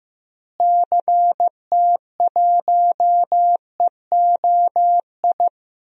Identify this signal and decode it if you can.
Morse code, heard as a keyed tone, about 15 wpm, 700 Hz. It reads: CT1EOI
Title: DXCC-100 Call Signs - 6 Characters - Repeated 3x - 15wpm